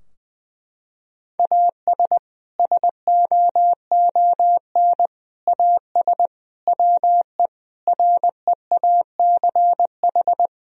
Morse code, heard as a keyed tone, about 21 wpm, 700 Hz. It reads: AS SOON AS WE REACH
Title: Sentences from Top 1000 Words - Review Part 2 - 20wpm